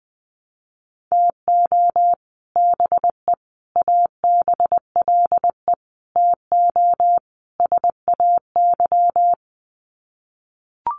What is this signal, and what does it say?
Morse code, heard as a keyed tone, about 20 wpm, 700 Hz. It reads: TO BE ABLE TO SAY
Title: Sets of 5 Words Encore2 20wpm